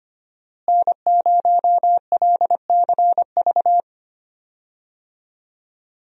Morse code, heard as a keyed tone, about 25 wpm, 700 Hz. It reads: N0LCV
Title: USA Call Signs - 5 Characters - Repeated 3x - 25wpm